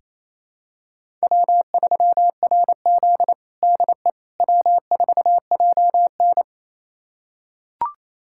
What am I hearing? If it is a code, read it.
Morse code, heard as a keyed tone, about 28 wpm, 700 Hz. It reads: W3RZ DE W4JN
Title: USA Callsign DE Callsign - 4 Characters - 28wpm